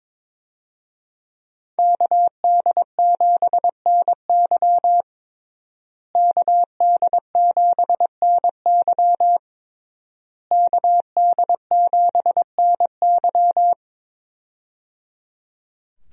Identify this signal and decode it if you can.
Morse code, heard as a keyed tone, about 22 wpm, 700 Hz. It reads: KD7NY KD7NY KD7NY
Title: USA Call Signs - 5 Characters - Repeated 3x - 22wpm